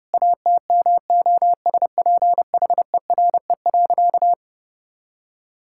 Morse code, heard as a keyed tone, about 30 wpm, 700 Hz. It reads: ATMOSPHERE.
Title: Sentences from Top 1600 Words 30wpm